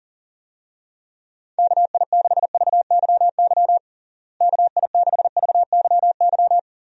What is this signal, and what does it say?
Morse code, heard as a keyed tone, about 37 wpm, 700 Hz. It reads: KI6VYY KI6VYY
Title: USA Call Signs - 6 Characters - Repeated 3x - 40wpm